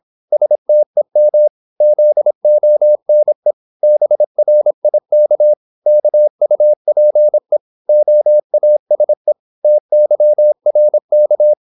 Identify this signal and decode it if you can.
Morse code, heard as a keyed tone, about 26 wpm, 600 Hz. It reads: STEM ZONE BRIK KUPE OASE TYRK
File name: Korte ord DK 26wpm